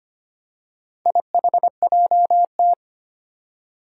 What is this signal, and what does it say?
Morse code, heard as a keyed tone, about 25 wpm, 700 Hz. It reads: IHJT
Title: Four Challenging Letter-Number Combinations 25wpm